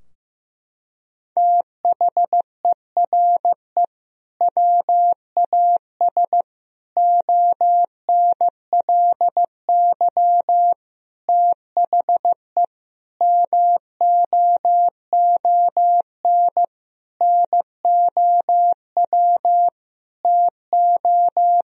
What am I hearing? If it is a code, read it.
Morse code, heard as a keyed tone, about 15 wpm, 700 Hz. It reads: THERE WAS ONLY THE MOON NOW TO
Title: Sentences from Top 1000 Words - Review Part 3 - 15wpm